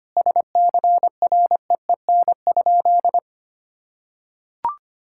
Morse code, heard as a keyed tone, about 25 wpm, 700 Hz. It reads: SCREEN?
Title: Sentences from Top 1000 Words - Review Part 6 - 25wpm